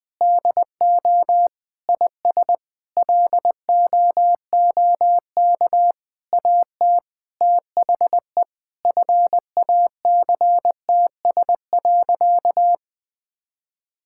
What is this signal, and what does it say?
Morse code, heard as a keyed tone, about 20 wpm, 700 Hz. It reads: DO IS LOOK AT THE FACTS.
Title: Sentences with 8-Word Phrases - 20wpm